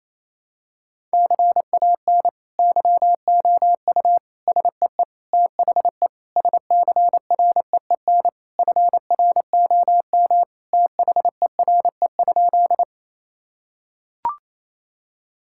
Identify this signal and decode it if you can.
Morse code, heard as a keyed tone, about 28 wpm, 700 Hz. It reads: CAN YOU SEE THE SCREEN FROM THERE?
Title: Sentences from Top 1000 Words - Review Part 6 - 28wpm